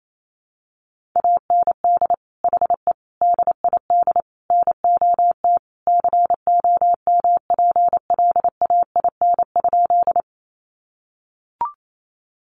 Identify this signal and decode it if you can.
Morse code, heard as a keyed tone, about 28 wpm, 700 Hz. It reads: AND HE DID NOT COMPLAIN?
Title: Sentences from Top 1300 Words 28wpm